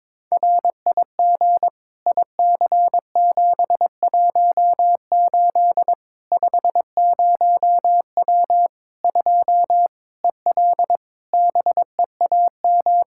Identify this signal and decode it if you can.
Morse code, heard as a keyed tone, about 22 wpm, 700 Hz. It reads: RIG IC718 50W 2 EL BEAM
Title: K1USN SST QSO Practice 22wpm